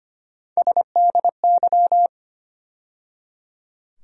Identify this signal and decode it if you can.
Morse code, heard as a keyed tone, about 25 wpm, 700 Hz. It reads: SDY
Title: USA Call Sign Postfixes - 3 Letters 25wpm